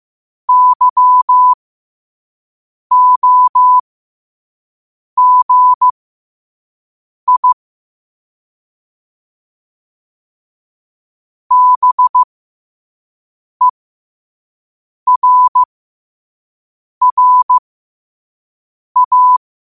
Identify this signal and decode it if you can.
Morse code, heard as a keyed tone, about 15 wpm, spaced as for 5 wpm, 1000 Hz. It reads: YOGI BERRA
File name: qotdfiles / 5wpm